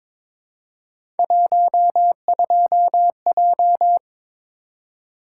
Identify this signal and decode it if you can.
Morse code, heard as a keyed tone, about 22 wpm, 700 Hz. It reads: 12J
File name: practice / 22wpm / Three Letter-Number Combinations - 22wpm